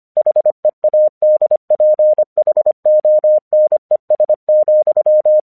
Morse code, heard as a keyed tone, about 24 wpm, 610 Hz. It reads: HEADPHONES,
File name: 1 English 25wpm / AQC27